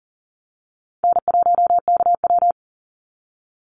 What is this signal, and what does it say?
Morse code, heard as a keyed tone, about 40 wpm, 700 Hz. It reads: N1KW
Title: Easy Call Signs - 40wpm